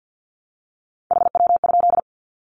Morse code, heard as a keyed tone, about 50 wpm, 700 Hz. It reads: HW?
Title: Common QSO Phrases 50wpm